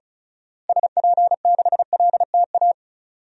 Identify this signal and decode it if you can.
Morse code, heard as a keyed tone, about 34 wpm, 700 Hz. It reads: SP6LTA
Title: DXCC-100 Call Signs - 6 Characters - Repeated 3x - 35wpm